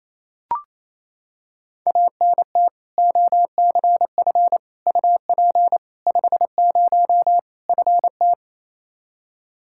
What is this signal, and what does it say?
Morse code, heard as a keyed tone, about 27 wpm, 700 Hz. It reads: ANT OCF UP 50 FT
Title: Practice QSO - Exchange 2 - Info - 28wpm